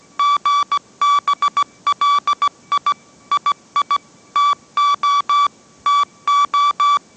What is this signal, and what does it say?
Morse code, heard as a keyed tone, about 17 wpm, 1150 Hz. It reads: GBLI II TO TO
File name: mrx / mrx080 18wpm